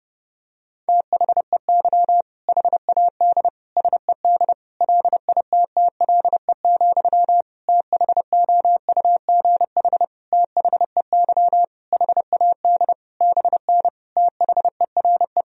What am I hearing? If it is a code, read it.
Morse code, heard as a keyed tone, about 30 wpm, 700 Hz. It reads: THEY HAD SED LITTLE, THOUGH THEY HAD BN THERE
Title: Sentences from Top 400 Words and Abbreviations 30wpm